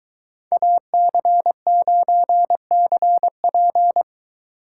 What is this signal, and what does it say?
Morse code, heard as a keyed tone, about 21 wpm, 700 Hz. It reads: AC9CP